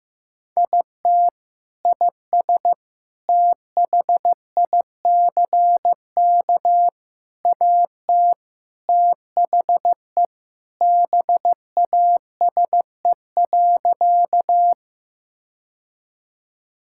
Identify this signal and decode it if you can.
Morse code, heard as a keyed tone, about 16 wpm, 700 Hz. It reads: IT IS THICK AT THE BASE.
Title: Sentences from Top 1600 Words 15wpm